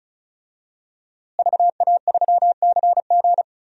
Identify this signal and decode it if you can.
Morse code, heard as a keyed tone, about 34 wpm, 700 Hz. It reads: VA3CG
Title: DXCC-100 Call Signs - 5 Characters - Repeated 2x - 35wpm